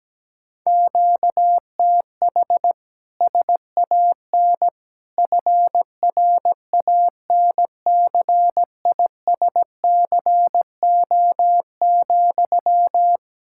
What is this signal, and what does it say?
Morse code, heard as a keyed tone, about 17 wpm, 700 Hz. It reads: QTH SAN FRANCISCO,
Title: Practice QSO - Exchange 1 - Beginning - 17wpm